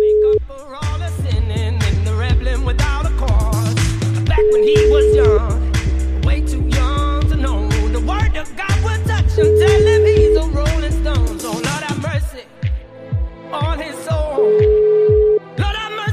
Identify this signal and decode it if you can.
Ringback tone (Europe, 425 Hz).